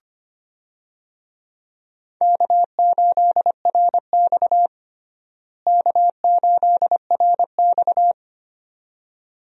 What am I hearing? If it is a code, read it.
Morse code, heard as a keyed tone, about 25 wpm, 700 Hz. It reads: K8RX K8RX
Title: USA Call Signs - 4 Characters - Repeated 2x - 25wpm